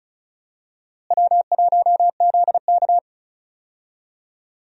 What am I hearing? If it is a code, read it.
Morse code, heard as a keyed tone, about 36 wpm, 700 Hz. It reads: W1ZK
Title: USA Call Signs - 4 Characters - Repeated 3x - 35wpm